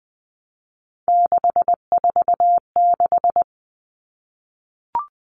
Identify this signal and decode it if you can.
Morse code, heard as a keyed tone, about 20 wpm, 700 Hz. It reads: 646
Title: Numbers - 3-Digit 20wpm